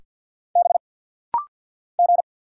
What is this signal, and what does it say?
Morse code, heard as a keyed tone, about 52 wpm, 700 Hz. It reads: B G
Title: Single Letters - Rapid-Fire 50wpm